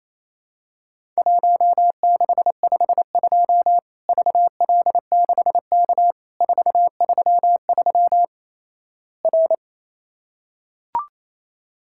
Morse code, heard as a keyed tone, about 27 wpm, 700 Hz. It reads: 1652 VL6K 433 R
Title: Worked All Europe DX Contest - QTC Exchange - 28wpm